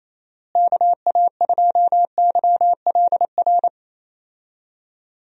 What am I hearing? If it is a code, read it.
Morse code, heard as a keyed tone, about 28 wpm, 700 Hz. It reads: KA2YLR
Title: USA Call Signs - 6 Characters - Repeated 3x - 28wpm